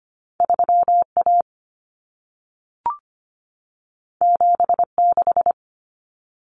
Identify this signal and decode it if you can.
Morse code, heard as a keyed tone, about 25 wpm, 700 Hz. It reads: 3A 76
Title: Common Cut Numbers - 2-Digit 25wpm